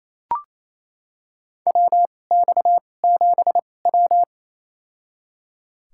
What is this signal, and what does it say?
Morse code, heard as a keyed tone, about 28 wpm, spaced as for 20 wpm, 700 Hz. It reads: WX7W
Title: USA Call Signs - 4 Characters - 2x Character Spacing - 28wpm